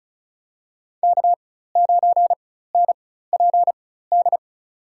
Morse code, heard as a keyed tone, about 36 wpm, spaced as for 16 wpm, 700 Hz. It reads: K9NPD
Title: USA Call Signs - 4x Character Spacing - 35wpm